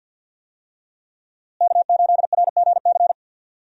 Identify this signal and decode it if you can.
Morse code, heard as a keyed tone, about 52 wpm, 700 Hz. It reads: K8RGC
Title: USA Call Signs - 5 Characters - Repeated 2x - 50wpm